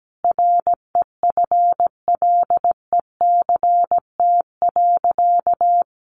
Morse code, heard as a keyed tone, about 17 wpm, 700 Hz. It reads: REFLECT.
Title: Sentences from Top 1500 Words 17wpm